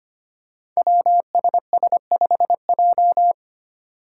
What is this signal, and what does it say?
Morse code, heard as a keyed tone, about 25 wpm, 700 Hz. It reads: WSS5J
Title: Five Challenging Letter-Number Combinations 25wpm